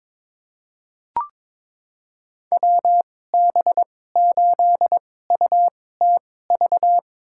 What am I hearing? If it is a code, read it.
Morse code, heard as a keyed tone, about 22 wpm, spaced as for 16 wpm, 700 Hz. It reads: WB8UTV
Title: USA Call Signs - 6 Characters - 2x Character Spacing - 22wpm